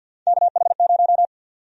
Morse code, heard as a keyed tone, about 51 wpm, 700 Hz. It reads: KH0
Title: USA Call Sign Prefixes - Repeated 2x - 50wpm